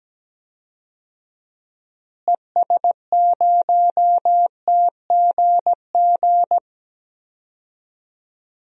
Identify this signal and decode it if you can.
Morse code, heard as a keyed tone, about 17 wpm, 700 Hz. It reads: ES0TGG
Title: DXCC-100 Call Signs - 6 Characters - Repeated 3x - 17wpm